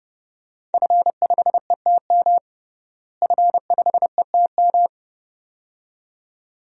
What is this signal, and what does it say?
Morse code, heard as a keyed tone, about 32 wpm, 700 Hz. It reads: F5ETM F5ETM
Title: DXCC-100 Call Signs - 5 Characters - Repeated 2x - 30wpm